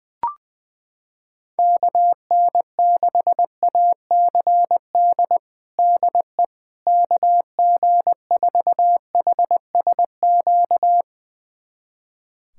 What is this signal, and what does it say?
Morse code, heard as a keyed tone, about 21 wpm, 700 Hz. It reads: KN6ACD DE KG4HSQ
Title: USA Callsign DE Callsign 20wpm